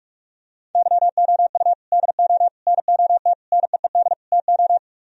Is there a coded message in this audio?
Yes: Morse code from 0.8 to 4.8 s, heard as a keyed tone, about 48 wpm, 700 Hz. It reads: YOU DO NOT NEED TO